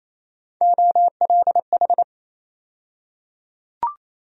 Morse code, heard as a keyed tone, about 28 wpm, 700 Hz. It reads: OLH
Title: USA Call Sign Postfixes - 3 Letters 28wpm